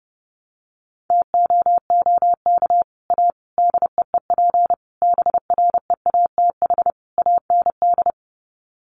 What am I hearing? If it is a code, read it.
Morse code, heard as a keyed tone, about 30 wpm, 700 Hz. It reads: TOOK A DEEP BREATH AND
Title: Sets of 5 Words Encore 30wpm